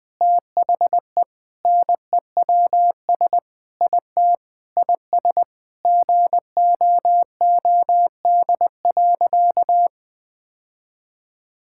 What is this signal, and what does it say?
Morse code, heard as a keyed tone, about 21 wpm, 700 Hz. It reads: THE NEWS IT IS GOOD.
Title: Sentences with 3-Word Phrases - 20wpm